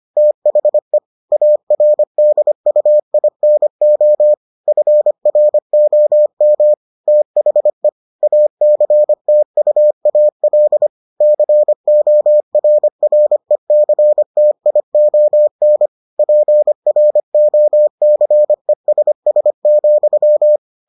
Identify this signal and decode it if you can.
Morse code, heard as a keyed tone, about 25 wpm, 600 Hz. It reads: THE ARDUINO FROM THE ACTUAL CORRECTION PROCESS,